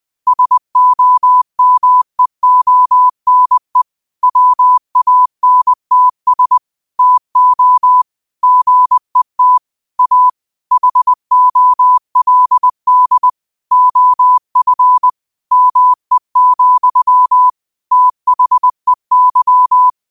Morse code, heard as a keyed tone, about 20 wpm, 1000 Hz. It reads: SOMEONE WANTS TO GET A HOLD OF ME, THEY
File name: qotdfiles / 20wpm